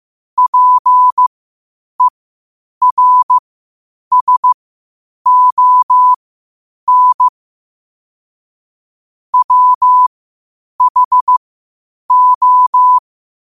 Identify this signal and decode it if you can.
Morse code, heard as a keyed tone, about 15 wpm, spaced as for 8 wpm, 1000 Hz. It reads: PERSON WHO
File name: qotdfiles / 8wpm